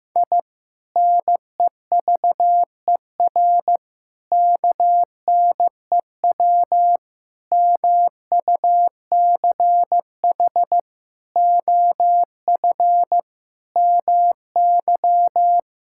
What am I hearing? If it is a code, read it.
Morse code, heard as a keyed tone, about 15 wpm, 700 Hz. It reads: I NEVER KNEW MUCH OF MY
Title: Sentences from Top 1000 Words - Review Part 6 - 15wpm